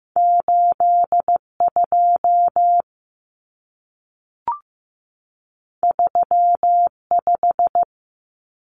Morse code, heard as a keyed tone, about 15 wpm, 700 Hz. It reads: 82 35
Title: Common Cut Numbers - 2-Digit 15wpm